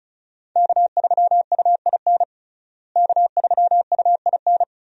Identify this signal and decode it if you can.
Morse code, heard as a keyed tone, about 36 wpm, 700 Hz. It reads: K3UIN K3UIN
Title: USA Call Signs - 5 Characters - Repeated 3x - 35wpm